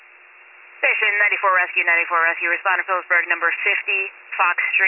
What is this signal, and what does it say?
Emergency Alert System (SAME) tones, heard as harsh buzzing bursts, with speech. It is heard from 1.0 to 3.7 s.